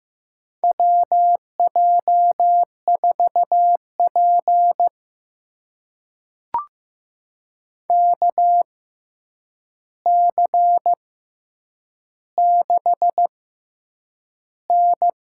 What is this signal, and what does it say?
Morse code, heard as a keyed tone, about 15 wpm, spaced as for 8 wpm, 700 Hz. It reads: WJ4P KC6N
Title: USA Call Signs - 4 Characters - 6x Character Spacing - 15wpm